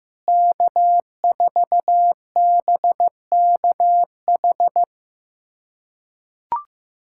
Morse code, heard as a keyed tone, about 15 wpm, 700 Hz. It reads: K4BKH
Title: USA Callsign DE Callsign 15wpm